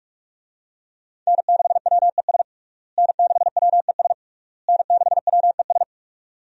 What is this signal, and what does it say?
Morse code, heard as a keyed tone, about 46 wpm, 700 Hz. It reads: N6WES N6WES N6WES
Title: USA Call Signs - 5 Characters - Repeated 3x - 45wpm